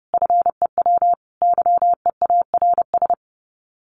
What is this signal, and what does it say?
Morse code, heard as a keyed tone, about 30 wpm, 700 Hz. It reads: FEW YEARS
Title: Sets of 3 Words 30wpm